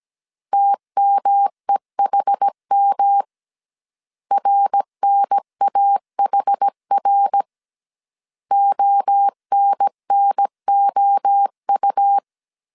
Morse code, heard as a keyed tone, about 17 wpm, 800 Hz. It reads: TMEHM RNAHR ONNOU